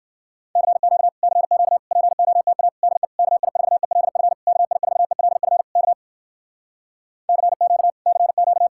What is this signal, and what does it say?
Morse code, heard as a keyed tone, about 68 wpm, 700 Hz. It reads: CQ CQ POTA DE KI4ELF KI4ELF K CQ CQ